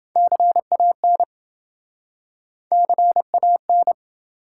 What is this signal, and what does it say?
Morse code, heard as a keyed tone, about 27 wpm, 700 Hz. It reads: CAN CAN